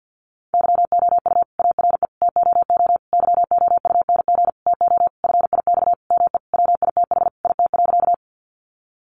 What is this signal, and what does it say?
Morse code, heard as a keyed tone, about 50 wpm, 700 Hz. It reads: YOU ARE TOO YOUNG TO FIX ME WITH IT.
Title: Sentences from Top 1400 Words 50wpm